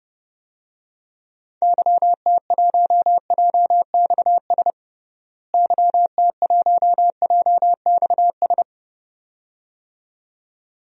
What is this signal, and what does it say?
Morse code, heard as a keyed tone, about 30 wpm, 700 Hz. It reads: YT1JXS YT1JXS
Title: DXCC-100 Call Signs - 6 Characters - Repeated 2x - 30wpm